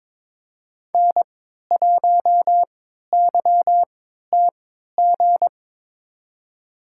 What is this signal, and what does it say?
Morse code, heard as a keyed tone, about 22 wpm, spaced as for 12 wpm, 700 Hz. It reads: N1YTG
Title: USA Call Signs - 5 Characters - 3x Character Spacing - 22wpm